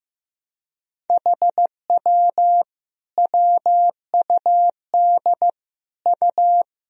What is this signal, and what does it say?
Morse code, heard as a keyed tone, about 15 wpm, 700 Hz. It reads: HW WUD U
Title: Sentences from Top 100 Words and Abbreviations 15wpm